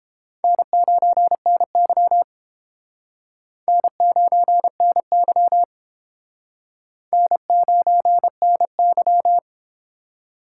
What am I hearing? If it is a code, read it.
Morse code, heard as a keyed tone, about 30 wpm, 700 Hz. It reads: N9NY N9NY N9NY